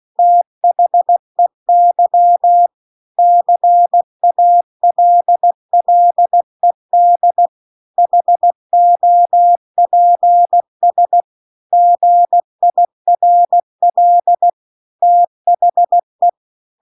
Morse code, heard as a keyed tone, about 16 wpm, 700 Hz. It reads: THEY CALLED HOPS GIRL THE